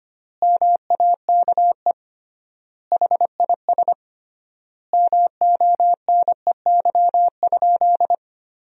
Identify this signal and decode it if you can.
Morse code, heard as a keyed tone, about 25 wpm, 700 Hz. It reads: MAKE HIS MONEY?
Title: 3x Word-Spacing - Sentences from Top 400 Words - 25wpm